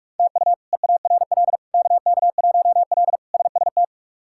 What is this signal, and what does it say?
Morse code, heard as a keyed tone, about 42 wpm, 700 Hz. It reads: TU EARL KK1L SST
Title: K1USN SST QSO Practice - 45wpm